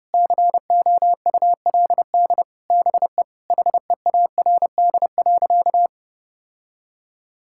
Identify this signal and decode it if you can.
Morse code, heard as a keyed tone, about 28 wpm, 700 Hz. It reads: COULD BE HEARD.
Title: Sentences from Top 1000 Words - Review Part 5 - 30wpm